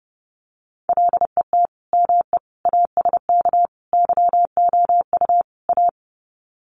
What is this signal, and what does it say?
Morse code, heard as a keyed tone, about 30 wpm, 700 Hz. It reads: LET ME ASK YOU A
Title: Sets of 5 Words Encore 30wpm